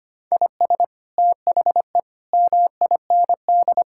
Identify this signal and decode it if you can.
Morse code, heard as a keyed tone, about 25 wpm, 700 Hz. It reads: IS THE MIND
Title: Sentences from Top 1000 Words - Review Part 6 - 25wpm